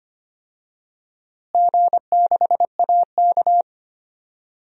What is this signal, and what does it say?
Morse code, heard as a keyed tone, about 25 wpm, 700 Hz. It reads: G6AK
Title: DXCC-100 Call Signs - 4 Characters - Repeated 3x - 25wpm